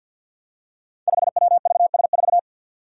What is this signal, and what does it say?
Morse code, heard as a keyed tone, about 52 wpm, 700 Hz. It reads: 5WVS4
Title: Five Challenging Letter-Number Combinations 50wpm